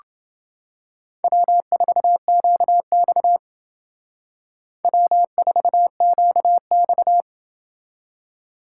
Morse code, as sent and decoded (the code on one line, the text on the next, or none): .-- ....- --.- -..- / .-- ....- --.- -..-
W4QX W4QX